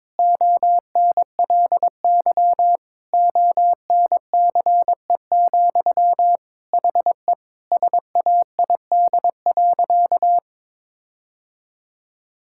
Morse code, heard as a keyed tone, about 22 wpm, 700 Hz. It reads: ONLY ONCE, HE SAID.
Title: Sentences from Top 1600 Words 22wpm